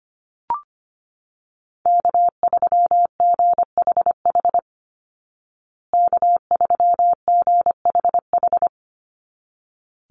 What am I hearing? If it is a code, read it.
Morse code, heard as a keyed tone, about 25 wpm, 700 Hz. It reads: K3GHH K3GHH
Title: Call Signs Repeated 3x - 25wpm